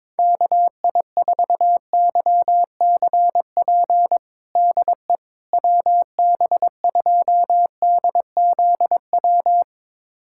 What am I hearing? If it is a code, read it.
Morse code, heard as a keyed tone, about 22 wpm, 700 Hz. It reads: KI4YCP DE WB2DZW
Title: USA Callsign DE Callsign - 6 Characters - 22wpm